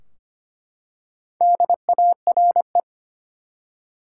Morse code, heard as a keyed tone, about 25 wpm, 700 Hz. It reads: DARE
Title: Four Letter Words - 25wpm